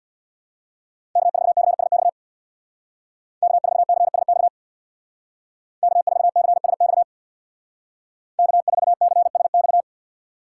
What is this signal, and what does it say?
Morse code, heard as a keyed tone, about 58 wpm, 700 Hz. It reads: K4CSX K4CSX K4CSX K4CSX
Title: Speed Racing - USA 1x3 Call Signs 50wpm